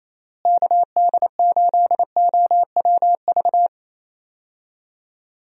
Morse code, heard as a keyed tone, about 28 wpm, 700 Hz. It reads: KD8OWV
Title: USA Call Signs - 6 Characters - Repeated 2x - 28wpm